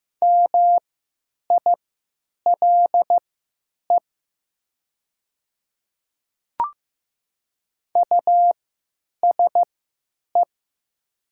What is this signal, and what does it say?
Morse code, heard as a keyed tone, about 15 wpm, spaced as for 8 wpm, 700 Hz. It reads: MILE USE
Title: 3x Character Spacing - Top 300 Words - 15wpm